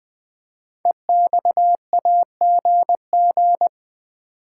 Morse code, heard as a keyed tone, about 20 wpm, 700 Hz. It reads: EXAGG
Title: Build A Word - 5 Letters At A Time - 10-Letter Words - 20wpm